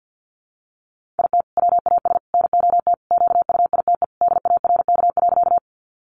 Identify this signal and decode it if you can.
Morse code, heard as a keyed tone, about 50 wpm, 700 Hz. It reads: IT WAS NOT QUITE DARK.
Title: Sentences from Top 700 Words 50wpm